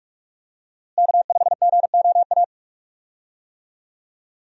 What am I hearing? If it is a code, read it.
Morse code, heard as a keyed tone, about 43 wpm, 700 Hz. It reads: K5GOA
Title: USA Call Signs - 5 Characters - Repeated 3x - 45wpm